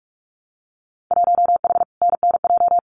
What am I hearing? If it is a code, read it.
Morse code, heard as a keyed tone, about 46 wpm, 700 Hz. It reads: 1H NNJ
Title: Winter Field Day Exchanges 45wpm